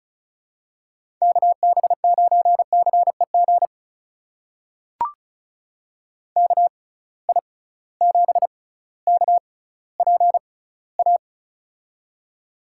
Morse code, heard as a keyed tone, about 36 wpm, spaced as for 18 wpm, 690 Hz. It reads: KB9CEG KI7KPA
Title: USA Call Signs - 6 Characters - 6x Character Spacing - 35wpm